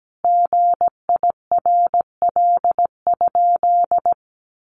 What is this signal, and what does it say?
Morse code, heard as a keyed tone, about 17 wpm, 700 Hz. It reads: GIRL?
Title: Sentences from Top 800 Words 17wpm